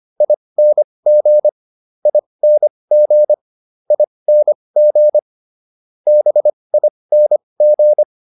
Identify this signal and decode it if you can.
Morse code, heard as a keyed tone, about 25 wpm, spaced as for 20 wpm, 600 Hz. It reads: ING ING ING BING